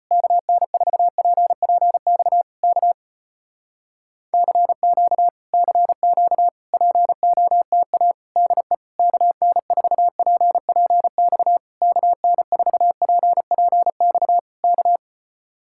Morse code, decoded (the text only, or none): KN4PPX K CQ CQ POTA DE KN4PPX KN4PPX K